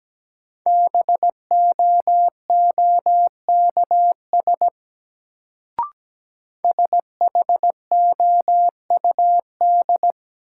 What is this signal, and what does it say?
Morse code, heard as a keyed tone, about 17 wpm, 700 Hz. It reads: BOOKS SHOUD
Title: Sentences from Top 1000 Words - Review Part 4 - New Word Drill - 17wpm